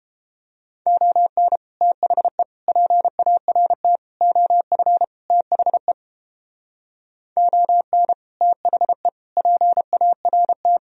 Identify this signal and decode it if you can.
Morse code, heard as a keyed tone, about 31 wpm, 700 Hz. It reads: ON THE PART OF THE ON THE PART